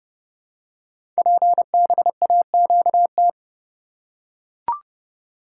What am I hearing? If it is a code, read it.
Morse code, heard as a keyed tone, about 30 wpm, 700 Hz. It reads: PBAQT
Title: Code Groups - 5 Letters - 30wpm